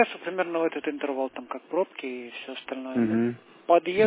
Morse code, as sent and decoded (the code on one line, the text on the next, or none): none